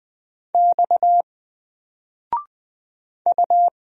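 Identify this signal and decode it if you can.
Morse code, heard as a keyed tone, about 20 wpm, 700 Hz. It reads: X U
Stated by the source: Single letter and number Morse drill at 20 wpm, from N4LCD.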